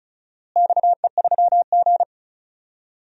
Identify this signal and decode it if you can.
Morse code, heard as a keyed tone, about 34 wpm, 700 Hz. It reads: XE3G
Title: DXCC-100 Call Signs - 4 Characters - Repeated 3x - 35wpm